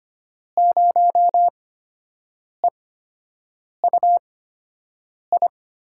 Morse code, heard as a keyed tone, about 25 wpm, spaced as for 7 wpm, 700 Hz. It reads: 0EUI
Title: USA Call Signs - 5 Characters - 8x Character Spacing - 25wpm